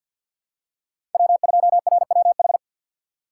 Morse code, heard as a keyed tone, about 51 wpm, 700 Hz. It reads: W2RWH
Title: USA Call Signs - 5 Characters - 6x Character Spacing - 50wpm